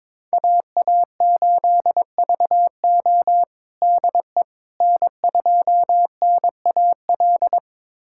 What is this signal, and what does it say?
Morse code, heard as a keyed tone, about 22 wpm, 700 Hz. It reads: AA8VO DE N2NAL
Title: USA Callsign DE Callsign - 5 Characters - 22wpm